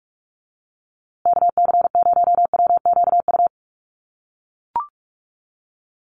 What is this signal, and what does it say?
Morse code, heard as a keyed tone, about 44 wpm, 700 Hz. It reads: KC0WQU
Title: USA Call Signs Repeated 2x 45wpm